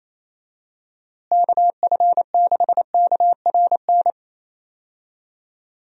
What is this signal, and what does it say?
Morse code, heard as a keyed tone, about 28 wpm, 700 Hz. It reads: KF6KRN
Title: USA Call Signs - 6 Characters - Repeated 2x - 28wpm